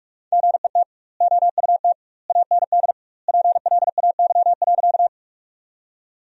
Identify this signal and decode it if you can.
Morse code, heard as a keyed tone, about 46 wpm, 700 Hz. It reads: GET OUT AND PLAY.
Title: 2x Word-Spacing - Sentences from Top 200 Words - 45wpm